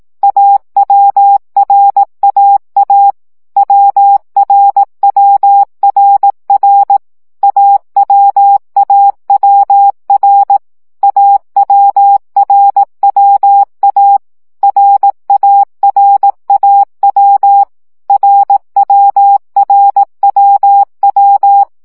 Morse code, heard as a keyed tone, about 18 wpm, 800 Hz. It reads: AWRAA WRWRR AWAWR AWRWA RARAW RWRWW
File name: CW gyakorlas a r w 15 baud